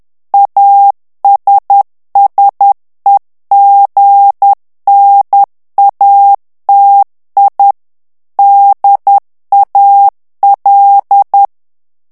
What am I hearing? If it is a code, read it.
Morse code, heard as a keyed tone, about 10 wpm, 800 Hz. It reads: ASSEGNATI DAL
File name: ITALIEN texte clair 2 vitesse 10 mots minute